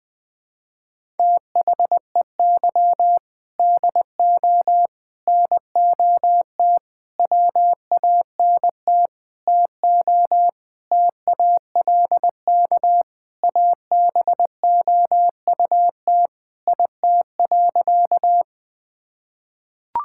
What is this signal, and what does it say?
Morse code, heard as a keyed tone, about 20 wpm, 700 Hz. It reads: THEY DO NOT WANT TO TALK ABOUT IT.
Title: Sentences with 5-Word Phrases - Spoken Beginning - 20wpm